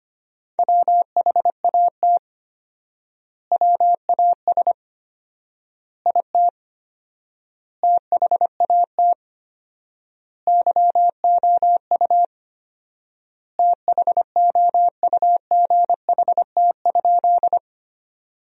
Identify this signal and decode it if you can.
Morse code, heard as a keyed tone, about 25 wpm, 700 Hz. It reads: WHAT WAS IT THAT YOU THOUGHT?
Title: 4x Word-Spacing - Sentences from Top 300 Words - 25wpm